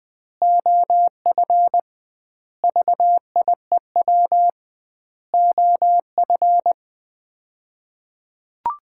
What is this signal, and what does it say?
Morse code, heard as a keyed tone, about 20 wpm, 700 Hz. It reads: OF VIEW OF
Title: 2x Word-Spacing - Sets of 5 Words - 20wpm